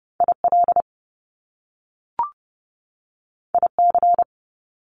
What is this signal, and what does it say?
Morse code, heard as a keyed tone, about 30 wpm, 700 Hz. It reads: IL IC
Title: Top Two Letter Combinations in Words 30wpm